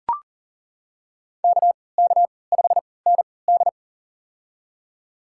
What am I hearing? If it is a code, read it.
Morse code, heard as a keyed tone, about 39 wpm, spaced as for 22 wpm, 700 Hz. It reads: KK5ND
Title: USA Call Signs - 3x Character Spacing - 40wpm